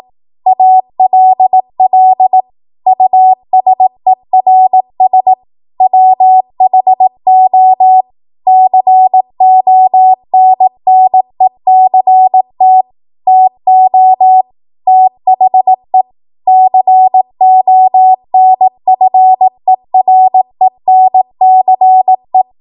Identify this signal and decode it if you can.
Morse code, heard as a keyed tone, about 18 wpm, 750 Hz. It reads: ALL USERS WHO CONNECT TO THE CONFERENCE
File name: Archive / 18 WPM / 18WPM